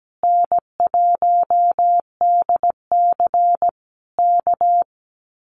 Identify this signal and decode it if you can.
Morse code, heard as a keyed tone, about 17 wpm, 700 Hz. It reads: N1DC K
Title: Common QSO Phrases Encore 17wpm